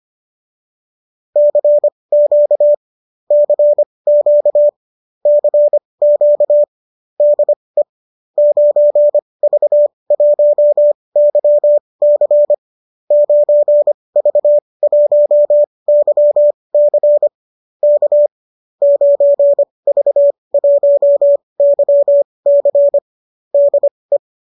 Morse code, heard as a keyed tone, about 25 wpm, spaced as for 20 wpm, 590 Hz. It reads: CQ CQ CQ DE 9V1YC 9V1YC K 9V1YC DE
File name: qso207 20